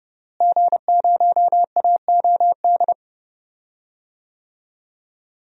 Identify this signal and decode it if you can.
Morse code, heard as a keyed tone, about 30 wpm, 700 Hz. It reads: G0AOD
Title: DXCC-100 Call Signs - 5 Characters - Repeated 2x - 30wpm